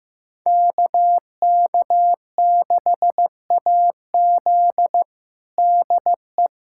Morse code, heard as a keyed tone, about 15 wpm, 700 Hz. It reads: KK6AZ DE
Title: USA Callsign DE Callsign 15wpm